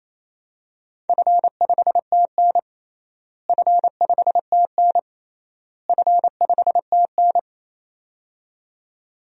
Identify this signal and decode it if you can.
Morse code, heard as a keyed tone, about 28 wpm, 700 Hz. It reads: F5TN F5TN F5TN
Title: DXCC-100 Call Signs - 4 Characters - Repeated 3x - 28wpm